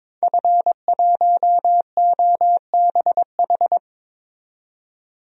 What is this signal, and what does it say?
Morse code, heard as a keyed tone, about 22 wpm, 700 Hz. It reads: F1OBH
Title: DXCC-100 Call Signs - 5 Characters - 22wpm